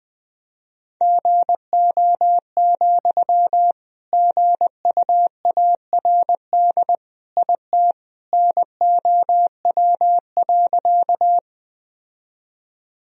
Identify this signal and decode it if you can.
Morse code, heard as a keyed tone, about 20 wpm, 700 Hz. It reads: GO, GUARD IT NOW.
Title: Sentences from Top 1000 Words - Review Part 4 - 20wpm